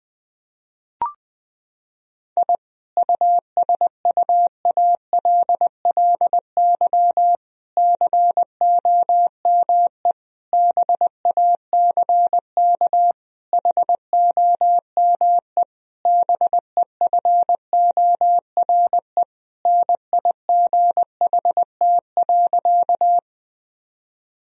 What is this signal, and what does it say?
Morse code, heard as a keyed tone, about 20 wpm, 700 Hz. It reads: I USUALLY COME BACK HOME BEFORE NIGHT.
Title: Sentences from Top 1000 Words - Review Part 3 - 20wpm